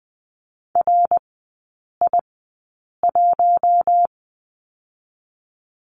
Morse code, heard as a keyed tone, about 20 wpm, spaced as for 8 wpm, 690 Hz. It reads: RI1
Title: 081 Lesson 14e - Number 2 - Three Characters 20wpm